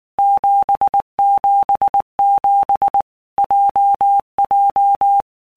Morse code, heard as a keyed tone, about 19 wpm, 800 Hz. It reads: 777 JJ
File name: All Groups 1-minute Blocks / Groups@18 / JZ17